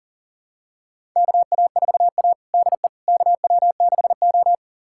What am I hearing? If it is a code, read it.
Morse code, heard as a keyed tone, about 40 wpm, 700 Hz. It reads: KA4A DE KW6O
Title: USA Callsign DE Callsign - 4 Characters - 40wpm